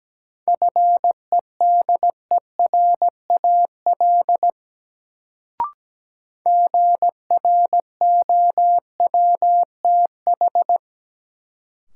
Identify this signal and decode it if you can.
Morse code, heard as a keyed tone, about 17 wpm, 700 Hz. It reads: FEDERAL GROWTH
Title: Sentences from Top 1000 Words - Review Part 5 - New Word Drill - 17wpm